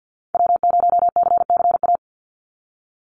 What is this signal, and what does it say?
Morse code, heard as a keyed tone, about 50 wpm, 700 Hz. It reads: W0CCA
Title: NA SOTA Activator Callsigns 50wpm